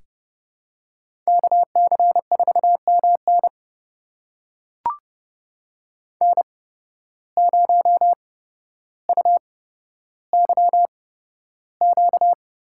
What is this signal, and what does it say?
Morse code, heard as a keyed tone, about 31 wpm, spaced as for 13 wpm, 700 Hz. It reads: KC4MN N0UYQ
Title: USA Call Signs - 5 Characters - 8x Character Spacing - 30wpm